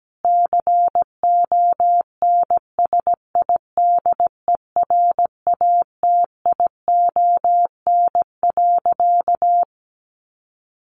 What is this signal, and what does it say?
Morse code, heard as a keyed tone, about 17 wpm, 700 Hz. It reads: CONSIDERATION.
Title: Sentences from Top 1500 Words 17wpm